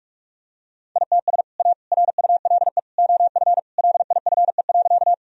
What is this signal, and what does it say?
Morse code, heard as a keyed tone, about 46 wpm, 700 Hz. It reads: ITS A RULE OF LIFE.
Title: Sentences from Top 1000 Words - Review Part 2 - 45wpm